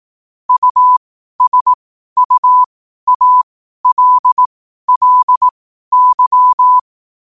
Morse code, heard as a keyed tone, about 18 wpm, spaced as for 12 wpm, 1000 Hz. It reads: USUALLY
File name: qotdfiles / 12wpm